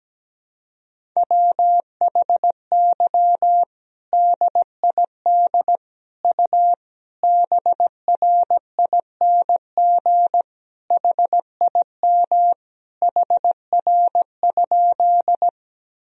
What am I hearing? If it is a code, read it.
Morse code, heard as a keyed tone, about 17 wpm, 700 Hz. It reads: WHY DID U BRING HIM HR?
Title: Sentences from Top 500 Words and Abbreviations 17wpm